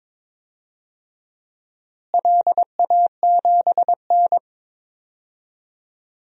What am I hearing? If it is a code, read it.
Morse code, heard as a keyed tone, about 21 wpm, 700 Hz. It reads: LA7N
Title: DXCC-100 Call Signs - 4 Characters - Repeated 3x - 22wpm